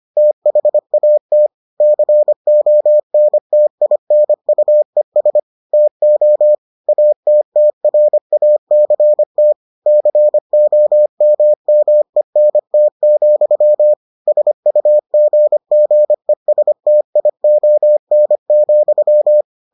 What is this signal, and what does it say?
Morse code, heard as a keyed tone, about 25 wpm, 600 Hz. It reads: THAT CONTINUES TO ATTRACT COMMENT, SUGGESTION,